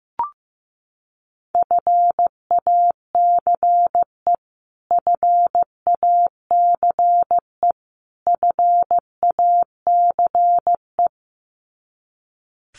Morse code, heard as a keyed tone, about 15 wpm, 700 Hz. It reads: FACE FACE FACE
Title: Top 300 Words Repeated 15wpm